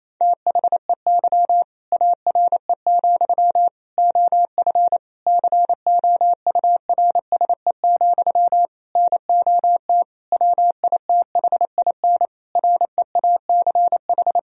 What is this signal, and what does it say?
Morse code, heard as a keyed tone, about 27 wpm, 700 Hz. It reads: THEY ARE, OF COURSE, NOT WITHIN REACH
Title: Sentences from Top 1000 Words - Review Part 4 - 28wpm